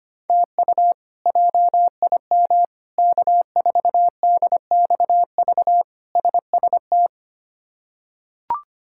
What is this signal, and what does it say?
Morse code, heard as a keyed tone, about 25 wpm, 700 Hz. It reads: TU JIM K4DXV SST
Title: K1USN SST QSO Practice - 25wpm